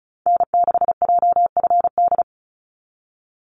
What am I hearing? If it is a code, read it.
Morse code, heard as a keyed tone, about 34 wpm, 700 Hz. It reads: N6JFD
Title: NA SOTA Activator Callsigns 35wpm